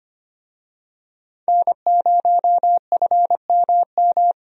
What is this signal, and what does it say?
Morse code, heard as a keyed tone, about 25 wpm, 700 Hz. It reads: N0FMM
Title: USA Call Signs - 5 Characters - Repeated 3x - 25wpm